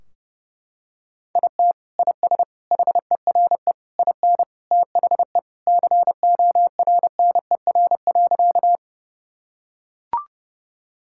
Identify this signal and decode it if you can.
Morse code, heard as a keyed tone, about 30 wpm, 700 Hz. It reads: IT IS HERE IN THE CORNER.
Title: Sentences from Top 1600 Words 30wpm